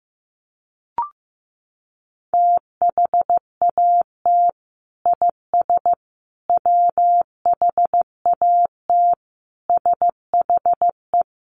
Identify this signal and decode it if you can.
Morse code, heard as a keyed tone, about 15 wpm, 700 Hz. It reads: THAT IS WHAT SHE
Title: Sentences from Top 100 Words 15wpm